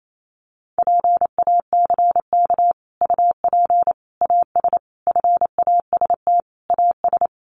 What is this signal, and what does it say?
Morse code, heard as a keyed tone, about 28 wpm, 700 Hz. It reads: PACK UP AS FAST AS
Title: Sentences from Top 1300 Words 28wpm